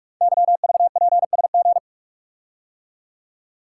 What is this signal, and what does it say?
Morse code, heard as a keyed tone, about 44 wpm, 700 Hz. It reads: YVPSG
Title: Code Groups - 5 Letters - 45wpm